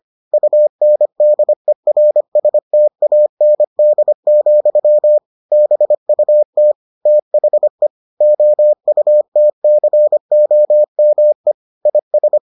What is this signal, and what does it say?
Morse code, heard as a keyed tone, about 25 wpm, 600 Hz. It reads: UNDERSTAND, BUT THE OUTCOME IS